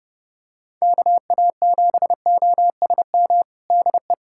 Morse code, heard as a keyed tone, about 32 wpm, 700 Hz. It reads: KA7OSM DE
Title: USA Callsign DE Callsign 30wpm